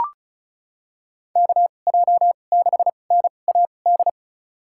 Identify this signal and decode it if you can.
Morse code, heard as a keyed tone, about 36 wpm, spaced as for 25 wpm, 700 Hz. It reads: KJ6NAD
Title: USA Call Signs - 6 Characters - 2x Character Spacing - 35wpm